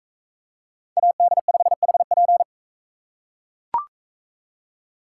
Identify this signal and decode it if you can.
Morse code, heard as a keyed tone, about 42 wpm, 700 Hz. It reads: AD5HP